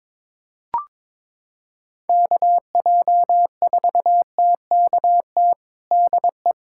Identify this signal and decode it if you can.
Morse code, heard as a keyed tone, about 22 wpm, 700 Hz. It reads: KJ4TKT DE
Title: USA Callsign DE Callsign 22wpm